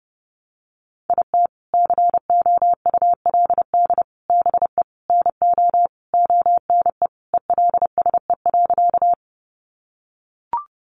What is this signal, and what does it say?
Morse code, heard as a keyed tone, about 28 wpm, 700 Hz. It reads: IT COULD BE NO ONE ELSE.
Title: Sentences from Top 800 Words 30wpm